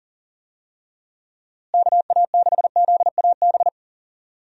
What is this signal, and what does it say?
Morse code, heard as a keyed tone, about 37 wpm, 700 Hz. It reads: KA6ZAB
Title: USA Call Signs - 6 Characters - Repeated 2x - 40wpm